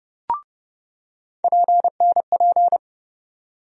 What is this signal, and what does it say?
Morse code, heard as a keyed tone, about 30 wpm, 700 Hz. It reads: PNP
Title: USA Call Sign Postfixes - 3 Letters 30wpm